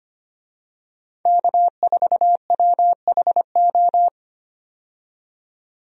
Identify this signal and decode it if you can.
Morse code, heard as a keyed tone, about 25 wpm, 700 Hz. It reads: K4WHO
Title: USA Call Signs - 5 Characters - 25wpm